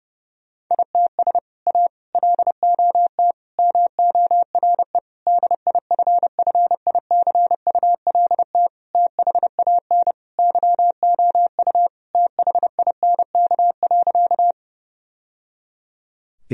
Morse code, heard as a keyed tone, about 30 wpm, 700 Hz. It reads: ITS A LOT MORE DIFFICULT THAN YOU THINK.
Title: Sentences from Top 1000 Words - Review Part 3 - 30wpm